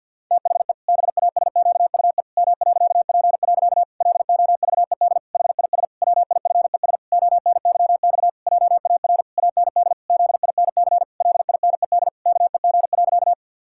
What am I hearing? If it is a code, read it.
Morse code, heard as a keyed tone, about 52 wpm, 700 Hz. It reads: THE BASQUE M.P. LOVED HIS WIFES ONYX JAR AND ZINC LINED KEG.
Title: Pangram Challenge v2 - 50wpm